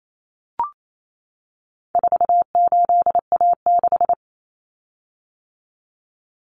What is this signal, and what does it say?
Morse code, heard as a keyed tone, about 28 wpm, 700 Hz. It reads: 48A6
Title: Common Cut Numbers - 4-Digit 28wpm